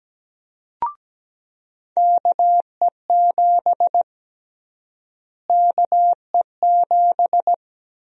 Morse code, heard as a keyed tone, about 17 wpm, 700 Hz. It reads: KE7 KE7
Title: USA Call Sign Prefixes - Repeated 2x - 17wpm